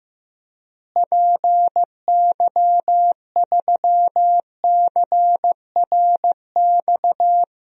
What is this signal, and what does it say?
Morse code, heard as a keyed tone, about 15 wpm, 700 Hz. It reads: PY3CRX
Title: DXCC-100 Call Signs - 6 Characters - Repeated 3x - 15wpm